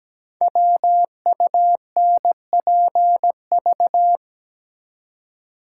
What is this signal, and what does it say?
Morse code, heard as a keyed tone, about 17 wpm, 700 Hz. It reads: WUNPV
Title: Code Groups - 5 Letters - 17wpm